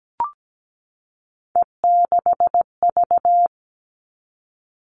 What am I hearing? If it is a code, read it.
Morse code, heard as a keyed tone, about 17 wpm, 700 Hz. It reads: E6V
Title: All Cut Numbers - 3-Digit 17wpm